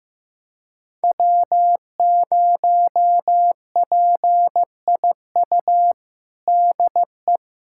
Morse code, heard as a keyed tone, about 15 wpm, 700 Hz. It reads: W0PIU DE
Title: USA Callsign DE Callsign - 5 Characters - 15wpm